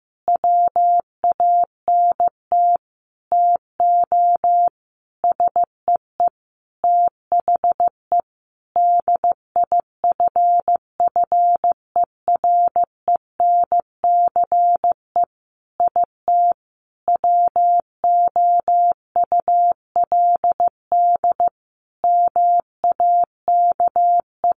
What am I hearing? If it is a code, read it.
Morse code, heard as a keyed tone, about 14 wpm, 700 Hz. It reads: WANT TO SEE THE DIFFERENCE IT WOULD MAKE
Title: Sentences from Top 1300 Words 15wpm